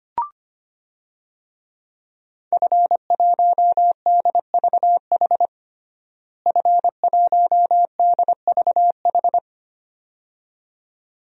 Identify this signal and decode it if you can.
Morse code, heard as a keyed tone, about 25 wpm, 700 Hz. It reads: F1DVH F1DVH
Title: DXCC-100 Call Signs - 5 Characters - Repeated 2x - 25wpm